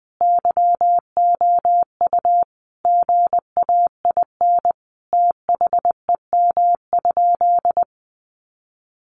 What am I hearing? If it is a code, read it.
Morse code, heard as a keyed tone, about 20 wpm, 700 Hz. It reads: YOU GAIN THEM?
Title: Sentences from Top 1000 Words 20wpm